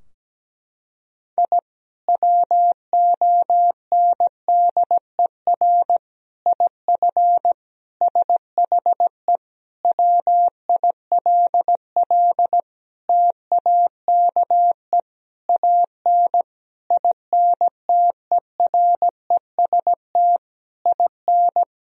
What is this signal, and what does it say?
Morse code, heard as a keyed tone, about 17 wpm, 700 Hz. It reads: I WONDER IF SHE WILL TAKE AN INTEREST IN
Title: Sentences from Top 1000 Words - Review Part 3 - 17wpm